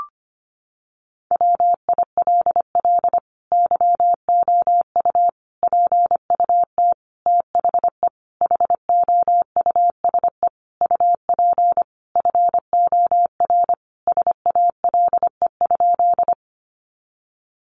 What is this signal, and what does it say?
Morse code, heard as a keyed tone, about 25 wpm, 700 Hz. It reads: WILL YOU PUT THE HOUSE UP FOR SALE?
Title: Sentences from Top 1400 Words 25wpm